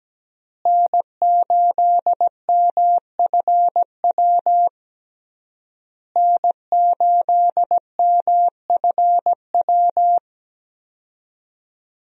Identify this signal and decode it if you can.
Morse code, heard as a keyed tone, about 17 wpm, 700 Hz. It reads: N8MFW N8MFW
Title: USA Call Signs - 5 Characters - Repeated 3x - 17wpm